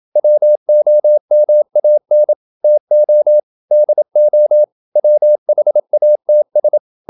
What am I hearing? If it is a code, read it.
Morse code, heard as a keyed tone, about 27 wpm, 600 Hz. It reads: WOMAN TO DO WHATS
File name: Never 0027wpm